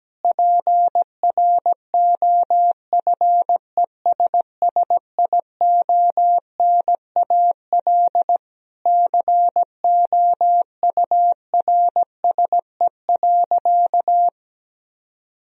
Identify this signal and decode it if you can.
Morse code, heard as a keyed tone, about 17 wpm, 700 Hz. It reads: PROFESSIONAL COURSE.
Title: Sentences from Top 1000 Words - Review Part 4 - 17wpm